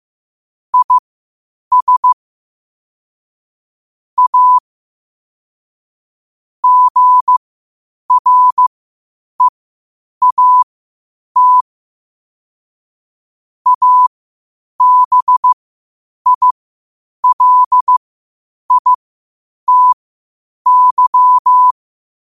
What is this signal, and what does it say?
Morse code, heard as a keyed tone, about 15 wpm, spaced as for 8 wpm, 1000 Hz. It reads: IS A GREAT ABILITY
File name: qotdfiles / 8wpm